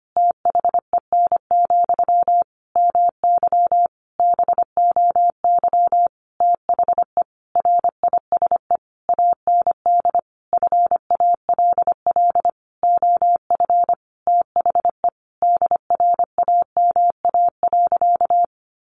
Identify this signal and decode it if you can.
Morse code, heard as a keyed tone, about 25 wpm, 700 Hz. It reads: THEN, MY BOY THE RISE AND FALL OF THE DRAMA.
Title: Sentences from Top 1500 Words 25wpm